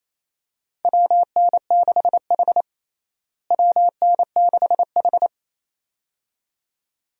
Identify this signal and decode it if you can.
Morse code, heard as a keyed tone, about 28 wpm, 700 Hz. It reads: WN6H WN6H
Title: USA Call Signs - 4 Characters - Repeated 2x - 28wpm